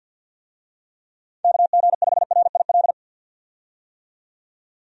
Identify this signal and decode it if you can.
Morse code, heard as a keyed tone, about 51 wpm, 700 Hz. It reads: KG5RIL
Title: USA Call Signs - 6 Characters - 50wpm